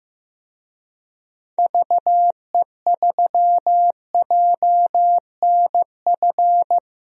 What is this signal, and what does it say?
Morse code, heard as a keyed tone, about 15 wpm, 700 Hz. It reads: VE3JNF
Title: DXCC-100 Call Signs - 6 Characters - 15wpm